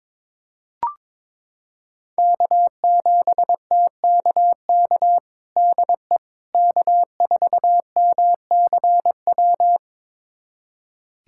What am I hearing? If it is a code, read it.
Morse code, heard as a keyed tone, about 22 wpm, 700 Hz. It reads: K7TKK DE K4MCW
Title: USA Callsign DE Callsign - 5 Characters - 22wpm